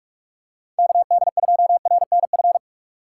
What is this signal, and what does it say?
Morse code, heard as a keyed tone, about 46 wpm, 700 Hz. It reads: KD2RNF
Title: USA Call Signs - 6 Characters - 2x Character Spacing - 45wpm